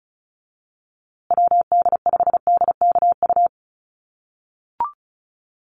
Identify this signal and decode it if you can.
Morse code, heard as a keyed tone, about 34 wpm, 700 Hz. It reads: WD5DKU
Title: USA Call Signs Repeated 3x 35wpm